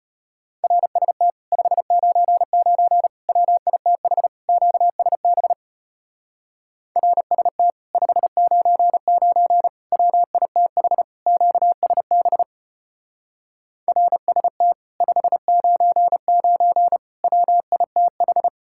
Morse code, heard as a keyed tone, about 33 wpm, 700 Hz. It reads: RST 599 WITH QSB RST 599 WITH QSB RST 599 WITH